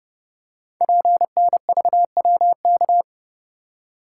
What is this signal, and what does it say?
Morse code, heard as a keyed tone, about 30 wpm, 700 Hz. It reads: PNVWK
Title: Code Groups - 5 Letters - 30wpm